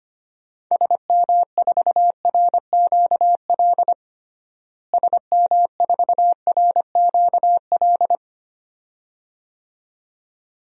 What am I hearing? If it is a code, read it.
Morse code, heard as a keyed tone, about 25 wpm, 700 Hz. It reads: SM4RQL SM4RQL
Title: DXCC-100 Call Signs - 6 Characters - Repeated 2x - 25wpm